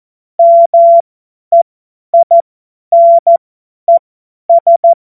Morse code, heard as a keyed tone, about 14 wpm, spaced as for 10 wpm, 680 Hz. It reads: MEINES
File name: Morsekurs / Witze / Witz 26. Oktober 2021, 50 F70